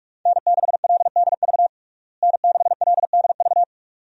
Morse code, heard as a keyed tone, about 41 wpm, 700 Hz. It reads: N6LDV N6LDV
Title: USA Call Signs - 5 Characters - Repeated 2x - 45wpm